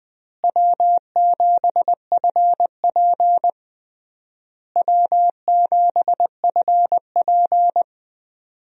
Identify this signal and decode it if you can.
Morse code, heard as a keyed tone, about 20 wpm, 700 Hz. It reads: W7FP W7FP
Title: USA Call Signs - 4 Characters - Repeated 3x - 20wpm